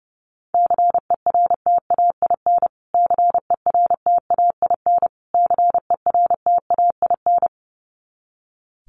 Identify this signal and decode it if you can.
Morse code, heard as a keyed tone, about 30 wpm, 700 Hz. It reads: CERTAIN CERTAIN CERTAIN
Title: Top 500 Words Repeated 30wpm